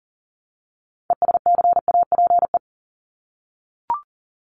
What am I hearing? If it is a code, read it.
Morse code, heard as a keyed tone, about 40 wpm, 700 Hz. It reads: ESCAPE
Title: Top 1501-1750 Words 40wpm